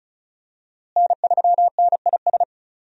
Morse code, heard as a keyed tone, about 36 wpm, 700 Hz. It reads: N3NIS
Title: USA Call Signs - 5 Characters - Repeated 3x - 35wpm